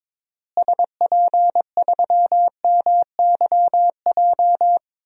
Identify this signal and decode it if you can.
Morse code, heard as a keyed tone, about 23 wpm, 700 Hz. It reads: SP3MYJ
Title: DXCC-100 Call Signs - 6 Characters - Repeated 3x - 22wpm